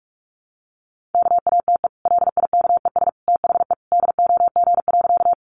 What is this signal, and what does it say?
Morse code, heard as a keyed tone, about 44 wpm, 700 Hz. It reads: KATE LIKES THE DOG.
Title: Easy Phrases 45wpm